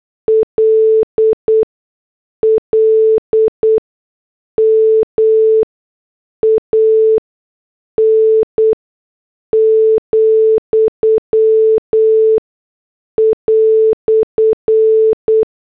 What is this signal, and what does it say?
Morse code, heard as a keyed tone, about 8 wpm, spaced as for 6 wpm, 440 Hz. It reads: LLMAN,"
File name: morse / headlines